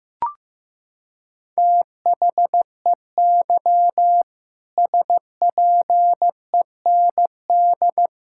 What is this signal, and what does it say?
Morse code, heard as a keyed tone, about 15 wpm, 700 Hz. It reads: THEY SPEND
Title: Sentences from Top 1000 Words - Review Part 5 - 15wpm